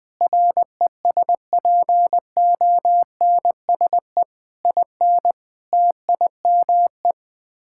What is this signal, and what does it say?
Morse code, heard as a keyed tone, about 20 wpm, 700 Hz. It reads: RESPONSE IN TIME
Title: Sets of 8 Words - 20wpm